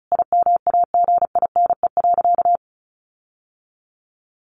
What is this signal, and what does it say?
Morse code, heard as a keyed tone, about 34 wpm, 700 Hz. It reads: IMAGINE.
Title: Sentences from Top 1400 Words 35wpm